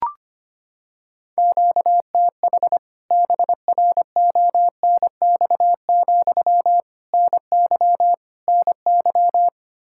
Morse code, heard as a keyed tone, about 25 wpm, 700 Hz. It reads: QTH BRONX, NY NY
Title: K1USN SST QSO Practice 25wpm